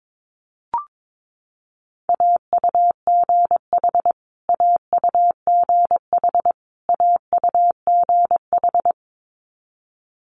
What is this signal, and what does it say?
Morse code, heard as a keyed tone, about 22 wpm, 700 Hz. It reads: AUGH AUGH AUGH
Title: Multi-letter Phonemes Repeated 3x 22wpm